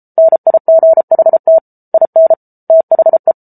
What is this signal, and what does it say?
Morse code, heard as a keyed tone, about 34 wpm, 650 Hz. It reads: NIGHT IN THE